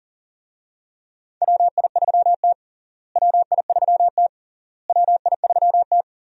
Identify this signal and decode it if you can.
Morse code, heard as a keyed tone, about 40 wpm, 700 Hz. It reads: WI3T WI3T WI3T
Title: USA Call Signs - 4 Characters - Repeated 3x - 40wpm